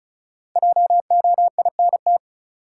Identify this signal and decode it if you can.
Morse code, heard as a keyed tone, about 36 wpm, 700 Hz. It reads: JOINT
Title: Sets of 5 Words Encore - New Word Drill - 35wpm